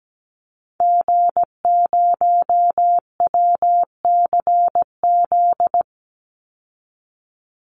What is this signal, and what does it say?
Morse code, heard as a keyed tone, about 17 wpm, 700 Hz. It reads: G0WCZ
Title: Call Signs Repeated 3x - 17wpm